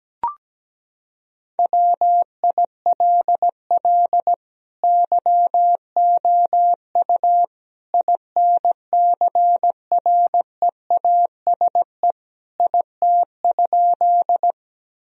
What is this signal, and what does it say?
Morse code, heard as a keyed tone, about 17 wpm, 700 Hz. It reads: WILL YOU INCREASE IT?
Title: Sentences from Top 1000 Words - Review Part 4 - 17wpm